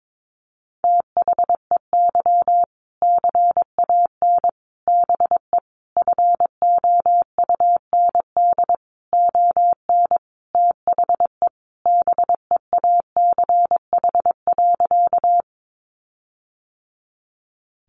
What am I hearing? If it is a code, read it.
Morse code, heard as a keyed tone, about 22 wpm, 700 Hz. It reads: THEY CAN BE FOUND ON THE BEACH.
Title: Sentences from Top 1500 Words 22wpm